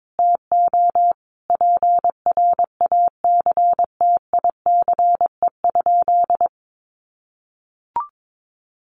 Morse code, heard as a keyed tone, about 22 wpm, 700 Hz. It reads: TO PRACTICE?
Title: Sentences from Top 700 Words 22wpm